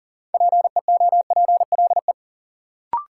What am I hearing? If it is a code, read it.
Morse code, heard as a keyed tone, about 40 wpm, 700 Hz. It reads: PEOPLE
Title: Sentences from Top 200 Words - New Word Drill - 40wpm